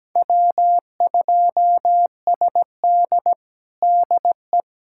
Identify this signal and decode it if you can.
Morse code, heard as a keyed tone, about 17 wpm, 700 Hz. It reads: W2SD DE
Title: USA Callsign DE Callsign - 4 Characters - 17wpm